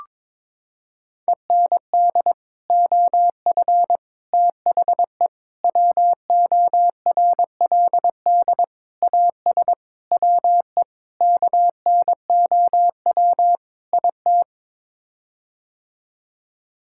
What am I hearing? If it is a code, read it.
Morse code, heard as a keyed tone, about 22 wpm, 700 Hz. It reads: END OF THE WORLD AS WE KNOW IT
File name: practice / 22wpm / Sets of 8 Words - 22wpm